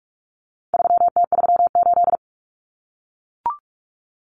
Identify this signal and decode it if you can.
Morse code, heard as a keyed tone, about 44 wpm, 700 Hz. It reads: 3T38
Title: Common Cut Numbers - 4-Digit 45wpm